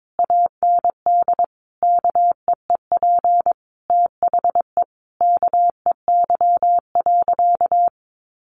Morse code, heard as a keyed tone, about 22 wpm, 700 Hz. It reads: AND KEEP THE KEY.
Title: Sentences from Top 1300 Words 22wpm